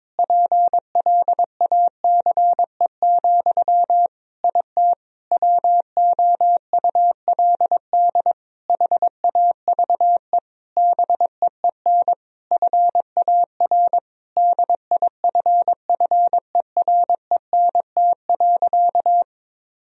Morse code, heard as a keyed tone, about 22 wpm, 700 Hz. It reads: PLACE, IT WOULD HAVE BEEN FAR DIFFERENT.
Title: Sentences from Top 1000 Words - Review Part 3 - 22wpm